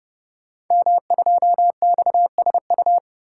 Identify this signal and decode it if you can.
Morse code, heard as a keyed tone, about 30 wpm, 700 Hz. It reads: M2XSU
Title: Code Groups - 5 Letter:Numbers - 30wpm